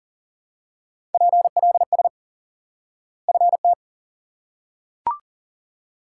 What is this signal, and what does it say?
Morse code, heard as a keyed tone, about 40 wpm, 700 Hz. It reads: PLS FT
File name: IQE Course 3 - Lesson 89 - YAGI - Two Elements - 40wpm